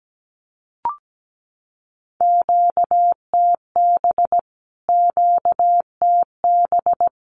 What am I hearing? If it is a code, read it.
Morse code, heard as a keyed tone, about 17 wpm, 700 Hz. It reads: QTB QTB
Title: Q-Codes Repeated 3x 17wpm